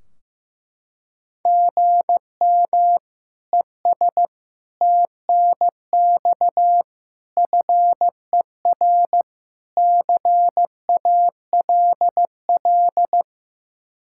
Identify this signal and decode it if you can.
Morse code, heard as a keyed tone, about 15 wpm, 700 Hz. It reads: GM ES TNX FER CALL
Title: Practice QSO - Exchange 1 - Beginning - 15wpm